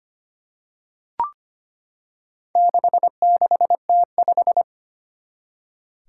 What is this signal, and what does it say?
Morse code, heard as a keyed tone, about 25 wpm, 700 Hz. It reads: 66T5
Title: Four Challenging Letter-Number Combinations 25wpm